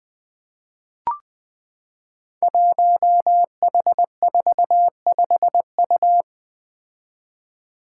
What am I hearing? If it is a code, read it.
Morse code, heard as a keyed tone, about 20 wpm, 700 Hz. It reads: 1H45U
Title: Five Challenging Letter-Number Combinations 20wpm